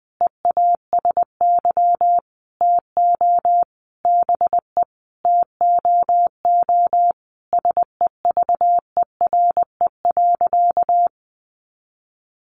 Morse code, heard as a keyed tone, about 20 wpm, 700 Hz. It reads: EASY TO BE TOO SEVERE.
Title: Sentences from Top 1400 Words 20wpm